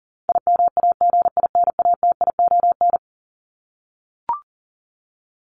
Morse code, heard as a keyed tone, about 40 wpm, 700 Hz. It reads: IMAGINATION
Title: Top 1001-1250 Words 40wpm